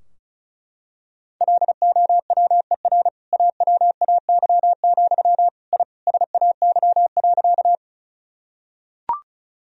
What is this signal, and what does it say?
Morse code, heard as a keyed tone, about 36 wpm, 700 Hz. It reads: LOWER AWAY, I SAY.
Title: Sentences from Top 1000 Words - Review Part 4 - 35wpm